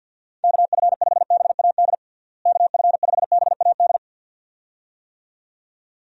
Morse code, heard as a keyed tone, about 52 wpm, 700 Hz. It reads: KF5BAD KF5BAD
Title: USA Call Signs - 6 Characters - Repeated 2x - 50wpm